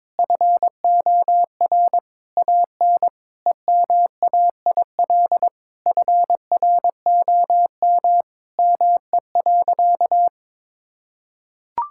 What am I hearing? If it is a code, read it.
Morse code, heard as a keyed tone, about 22 wpm, 700 Hz. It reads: FOR AN EMAIL FROM ME.
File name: Sentences with 7-Word Phrases - 22wpm